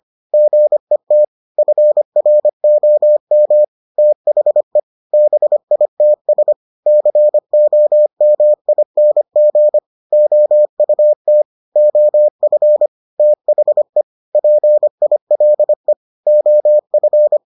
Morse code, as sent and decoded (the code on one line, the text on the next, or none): --. - / ..-. .-. --- -- / - .... / -... .. - ... / -.-. --- -- .. -. --. / --- ..- - / --- ..-. / - .... / .--. .. .-.. / --- ..-.
GET FROM THE BITS COMING OUT OF THE PILE OF